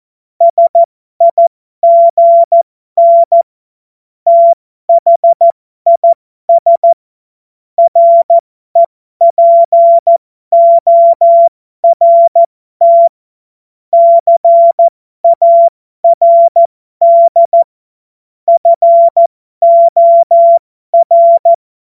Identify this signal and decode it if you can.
Morse code, heard as a keyed tone, about 14 wpm, spaced as for 12 wpm, 680 Hz. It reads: SIGN THIS REPORT CARD FOR
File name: Morsekurs / Witze